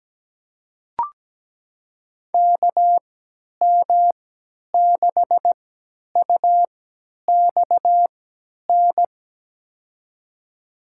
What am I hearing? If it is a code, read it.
Morse code, heard as a keyed tone, about 17 wpm, spaced as for 10 wpm, 700 Hz. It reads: KM6UXN
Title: USA Call Signs - 6 Characters - 3x Character Spacing - 17wpm